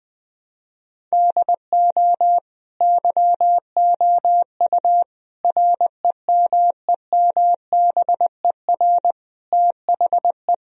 Morse code, heard as a keyed tone, about 20 wpm, 700 Hz. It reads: DO YOU REMEMBER THE
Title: Sentences from Top 1000 Words - Review Part 1 - 20wpm